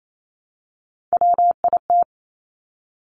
Morse code, heard as a keyed tone, about 28 wpm, 700 Hz. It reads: WIT
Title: Top Three Letter Combinations in Words 28wpm